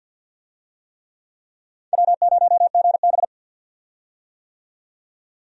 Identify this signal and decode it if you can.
Morse code, heard as a keyed tone, about 47 wpm, 700 Hz. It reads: W0GB
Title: USA Call Signs - 4 Characters - 50wpm